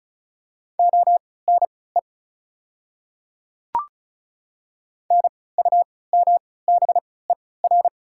Morse code, heard as a keyed tone, about 36 wpm, spaced as for 20 wpm, 690 Hz. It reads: ONE NUMBER
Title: 3x Character Spacing - Top 300 Words - 35wpm